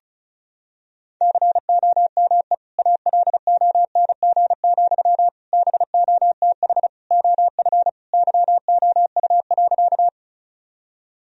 Morse code, heard as a keyed tone, about 36 wpm, 700 Hz. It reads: COME ALONG, BOTH OF YOU.
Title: Sentences from Top 1000 Words - Review Part 1 - 35wpm